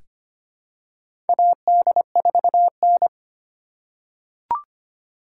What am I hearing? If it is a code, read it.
Morse code, heard as a keyed tone, about 25 wpm, 700 Hz. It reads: AD4N
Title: USA Call Signs - 4 Characters - 3x Character Spacing - 25wpm